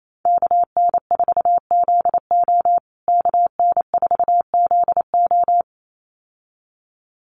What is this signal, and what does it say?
Morse code, heard as a keyed tone, about 28 wpm, 700 Hz. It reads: KN4ZO KN4ZO
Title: USA Call Signs Repeated 2x 28wpm